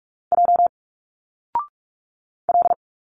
Morse code, heard as a keyed tone, about 46 wpm, 700 Hz. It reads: J L
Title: Single Letters 45wpm